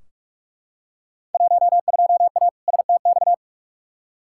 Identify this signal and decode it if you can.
Morse code, heard as a keyed tone, about 44 wpm, 700 Hz. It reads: 12A STX
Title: Field Day Exchanges Encore - 45wpm